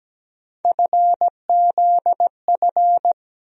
Morse code, heard as a keyed tone, about 17 wpm, 700 Hz. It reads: FZF
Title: USA Call Sign Postfixes - 3 Letters 17wpm